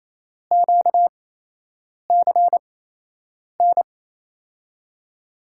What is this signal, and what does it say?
Morse code, heard as a keyed tone, about 28 wpm, spaced as for 8 wpm, 700 Hz. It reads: QCN
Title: USA Call Signs - 6 Characters - 8x Character Spacing - 28wpm